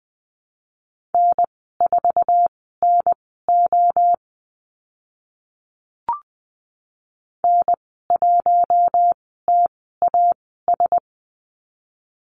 Morse code, heard as a keyed tone, about 20 wpm, spaced as for 14 wpm, 700 Hz. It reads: N4NO N1TAS
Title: 048 Lesson 9g - Number 4 - Call Signs 20wpm